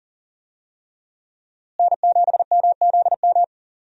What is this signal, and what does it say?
Morse code, heard as a keyed tone, about 40 wpm, 700 Hz. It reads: N7MZM
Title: USA Call Signs - 5 Characters - Repeated 3x - 40wpm